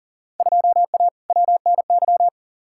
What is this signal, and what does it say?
Morse code, heard as a keyed tone, about 40 wpm, 700 Hz. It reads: 2A WNY
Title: Field Day Exchanges - 40wpm